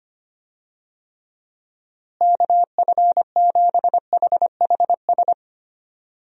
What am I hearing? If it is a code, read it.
Morse code, heard as a keyed tone, about 25 wpm, 700 Hz. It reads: KF7HHS
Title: USA Call Signs - 6 Characters - Repeated 2x - 25wpm